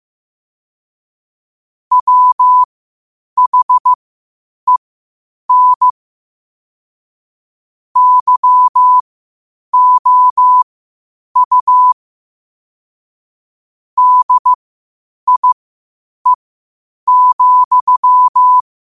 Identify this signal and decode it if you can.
Morse code, heard as a keyed tone, about 15 wpm, spaced as for 8 wpm, 1000 Hz. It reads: WHEN YOU DIE,
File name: qotdfiles / 8wpm